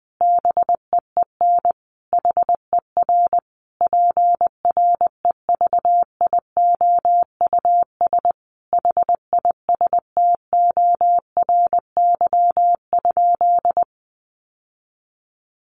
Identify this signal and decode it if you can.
Morse code, heard as a keyed tone, about 19 wpm, 700 Hz. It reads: BEEN HER PREVIOUS HISTORY?
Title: Sentences from Top 1500 Words 20wpm